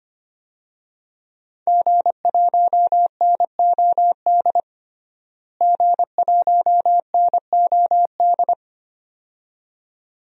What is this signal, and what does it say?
Morse code, heard as a keyed tone, about 25 wpm, 700 Hz. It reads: G1NOD G1NOD
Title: DXCC-100 Call Signs - 5 Characters - Repeated 2x - 25wpm